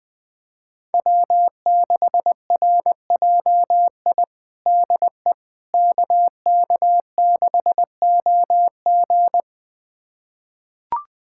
Morse code, heard as a keyed tone, about 20 wpm, 700 Hz. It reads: W6RJI DE KK6OG
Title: USA Callsign DE Callsign - 5 Characters - 20wpm